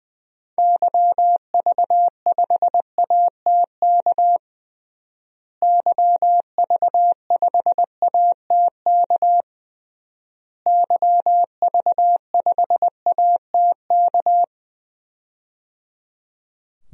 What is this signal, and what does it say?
Morse code, heard as a keyed tone, about 20 wpm, 700 Hz. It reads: YV5ATK YV5ATK YV5ATK
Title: DXCC-100 Call Signs - 6 Characters - Repeated 3x - 20wpm